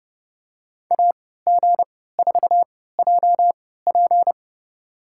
Morse code, heard as a keyed tone, about 32 wpm, spaced as for 17 wpm, 700 Hz. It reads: AG4JP
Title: USA Call Signs - 3x Character Spacing - 30wpm